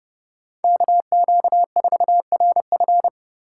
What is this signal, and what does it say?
Morse code, heard as a keyed tone, about 30 wpm, 700 Hz. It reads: KQ4RF
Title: USA Call Signs - 5 Characters - 2x Character Spacing - 30wpm